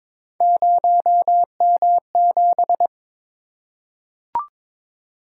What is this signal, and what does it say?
Morse code, heard as a keyed tone, about 22 wpm, 700 Hz. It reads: 0M7
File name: practice / 22wpm / Three Letter-Number Combinations - 22wpm